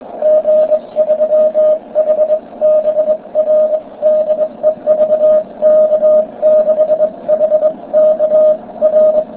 Morse code, heard as a keyed tone, about 21 wpm, 620 Hz. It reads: G3HBR DE VK6HK R